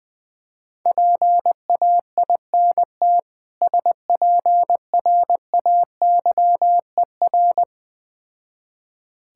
Morse code, heard as a keyed tone, about 20 wpm, 700 Hz. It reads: PAINT SPRAYER
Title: Difficult Words with Hints - 20wpm